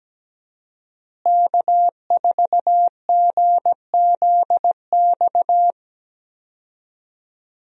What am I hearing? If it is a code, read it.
Morse code, heard as a keyed tone, about 17 wpm, 700 Hz. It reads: K4GZX
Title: USA Call Signs - 5 Characters - Repeated 2x - 17wpm